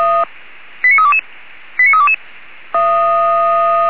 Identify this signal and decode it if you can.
Five-tone selective call (ZVEI), heard as quick runs of five tones, tones of about 70 ms. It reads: 89322 89322